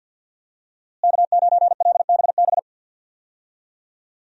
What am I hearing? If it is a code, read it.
Morse code, heard as a keyed tone, about 46 wpm, 700 Hz. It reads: K9LBB
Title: USA Call Signs - 5 Characters - Repeated 2x - 50wpm